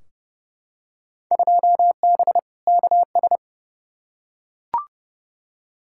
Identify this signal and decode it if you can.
Morse code, heard as a keyed tone, about 30 wpm, 700 Hz. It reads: 2B KS
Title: Field Day Exchanges Encore - 30wpm